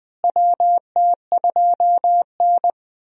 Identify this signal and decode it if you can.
Morse code, heard as a keyed tone, about 20 wpm, 700 Hz. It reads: WT2N
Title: USA Call Signs - 4 Characters - Repeated 2x - 20wpm